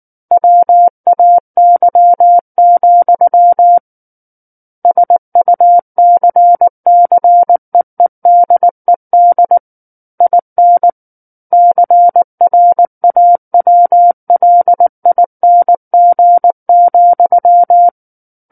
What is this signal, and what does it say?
Morse code, heard as a keyed tone, about 19 wpm, 700 Hz. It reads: WAY, SUCCEEDED IN CRAWLING,
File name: Book One / 10-Chapter 10 - 19 WPM